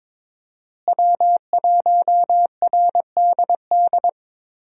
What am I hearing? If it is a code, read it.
Morse code, heard as a keyed tone, about 21 wpm, 700 Hz. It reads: W1RDD
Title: USA Call Signs - 5 Characters - Repeated 3x - 22wpm